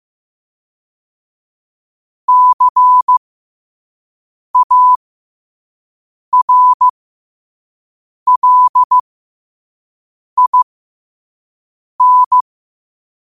Morse code, heard as a keyed tone, about 15 wpm, spaced as for 5 wpm, 1000 Hz. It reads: CARLIN
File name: qotdfiles / 5wpm